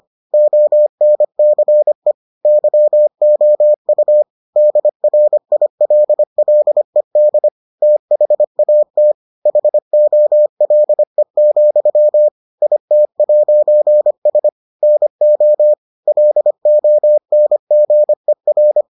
Morse code, heard as a keyed tone, about 25 wpm, 600 Hz. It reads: ONCE YOU DRILLED THAT HOLE, IT'S NO LONGER